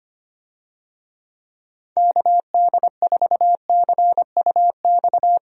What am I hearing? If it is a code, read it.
Morse code, heard as a keyed tone, about 25 wpm, 700 Hz. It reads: KD4CUX
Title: USA Call Signs - 6 Characters - Repeated 2x - 25wpm